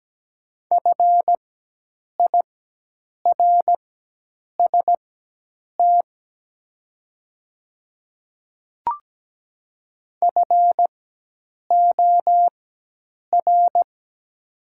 Morse code, heard as a keyed tone, about 17 wpm, spaced as for 8 wpm, 700 Hz. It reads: FIRST FOR
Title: 4x Character Spacing - Top 100 Words - 17wpm